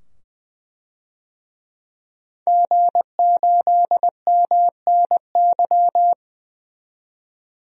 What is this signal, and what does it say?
Morse code, heard as a keyed tone, about 20 wpm, 700 Hz. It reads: G8MNY
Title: DXCC-100 Call Signs - 5 Characters - 20wpm